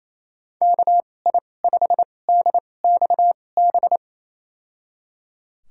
Morse code, heard as a keyed tone, about 28 wpm, spaced as for 20 wpm, 700 Hz. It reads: KI5DXB
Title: USA Call Signs - 6 Characters - 2x Character Spacing - 28wpm